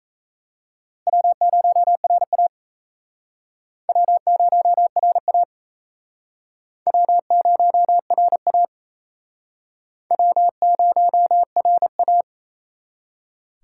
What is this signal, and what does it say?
Morse code, heard as a keyed tone, about 38 wpm, 700 Hz. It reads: W0RA W0RA W0RA W0RA